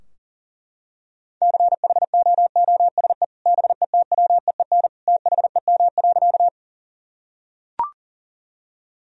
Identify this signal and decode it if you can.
Morse code, heard as a keyed tone, about 37 wpm, 700 Hz. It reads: CHOOSE BETWEEN THEM.
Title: Sentences from Top 1000 Words - Review Part 4 - 40wpm